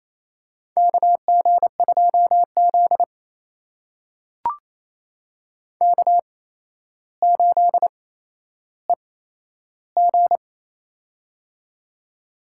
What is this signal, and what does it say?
Morse code, heard as a keyed tone, about 28 wpm, spaced as for 12 wpm, 700 Hz. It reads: KG2Z K8EG
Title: USA Call Signs - 4 Characters - 8x Character Spacing - 28wpm